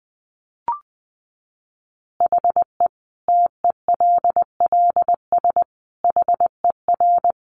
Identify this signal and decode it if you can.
Morse code, heard as a keyed tone, about 20 wpm, 700 Hz. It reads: HE TELLS HER
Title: Additional Easy Phrases 20wpm